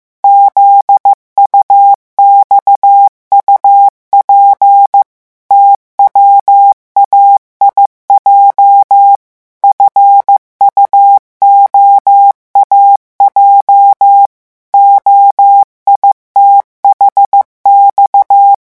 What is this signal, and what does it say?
Morse code, heard as a keyed tone, about 15 wpm, 800 Hz. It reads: ZUXUP TWAIJ FUOAJ OITHX
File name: morse mp3 / revision8-vitesse 14 mots